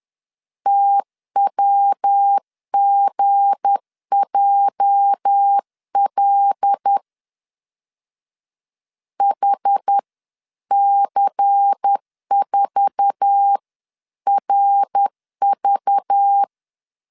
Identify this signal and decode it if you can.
Morse code, heard as a keyed tone, about 10 wpm, 800 Hz. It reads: TWGJL HC4RV
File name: CW / DET / 1 9 1500C S A 10wpm 4511